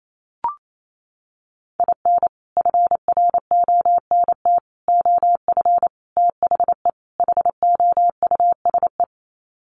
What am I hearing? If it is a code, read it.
Morse code, heard as a keyed tone, about 28 wpm, 700 Hz. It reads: IN FRONT OF THE HOUSE
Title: Sets of 5 Words Encore2 28wpm